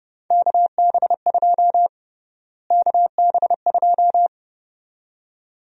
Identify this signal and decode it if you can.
Morse code, heard as a keyed tone, about 31 wpm, 700 Hz. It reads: KB2 KB2
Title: USA Call Sign Prefixes - Repeated 2x - 30wpm